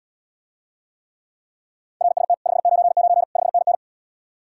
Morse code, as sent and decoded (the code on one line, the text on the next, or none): .-. ... - / ..... ----. ----. / ..... -. -.
RST 599 5NN